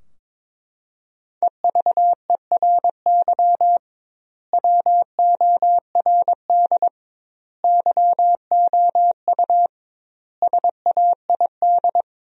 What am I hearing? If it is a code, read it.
Morse code, heard as a keyed tone, about 22 wpm, 700 Hz. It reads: EVERY WORD YOU SAID
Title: 2x Word-Spacing - Sentences from Top 400 Words - 22wpm